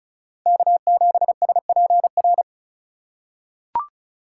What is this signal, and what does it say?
Morse code, heard as a keyed tone, about 36 wpm, 700 Hz. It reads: K7SPR
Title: USA Call Signs - 5 Characters - 4x Character Spacing - 35wpm